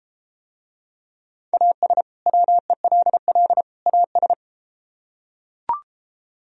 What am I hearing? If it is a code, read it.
Morse code, heard as a keyed tone, about 33 wpm, 700 Hz. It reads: AS WELL AS